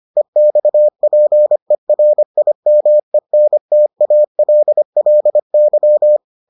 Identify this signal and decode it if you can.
Morse code, heard as a keyed tone, about 25 wpm, 600 Hz. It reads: EXPERIMENTALLY